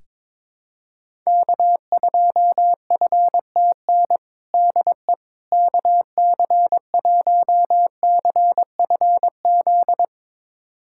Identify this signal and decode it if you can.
Morse code, heard as a keyed tone, about 22 wpm, 700 Hz. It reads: K2FTN DE KC1CFZ
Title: USA Callsign DE Callsign 22wpm